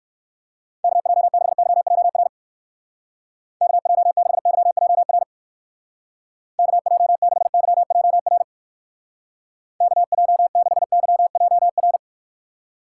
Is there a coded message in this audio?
Yes: Morse code from 0.8 to 12.0 s, heard as a keyed tone, about 58 wpm, 700 Hz. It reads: KJ6YJR KJ6YJR KJ6YJR KJ6YJR